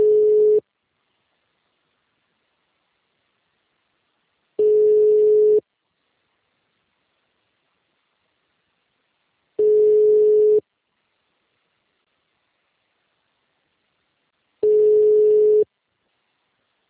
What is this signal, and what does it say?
Ringback tone (Europe, 425 Hz).
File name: out- -2061- - - .30477